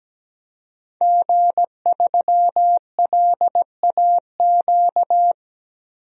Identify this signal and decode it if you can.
Morse code, heard as a keyed tone, about 17 wpm, 700 Hz. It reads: G3LAQ
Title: DXCC-100 Call Signs - 5 Characters - 17wpm